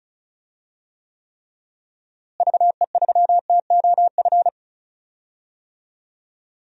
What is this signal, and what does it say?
Morse code, heard as a keyed tone, about 36 wpm, 700 Hz. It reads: VE3TOF
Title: DXCC-100 Call Signs - 6 Characters - Repeated 3x - 35wpm